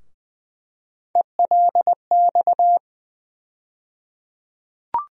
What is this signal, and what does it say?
Morse code, heard as a keyed tone, about 20 wpm, 700 Hz. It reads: ELX
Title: USA Call Sign Postfixes - 3 Letters 20wpm